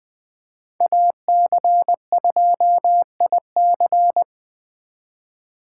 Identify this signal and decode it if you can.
Morse code, heard as a keyed tone, about 20 wpm, 700 Hz. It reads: AC2IC
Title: Speed Racing - USA 2x2 Call Signs 17wpm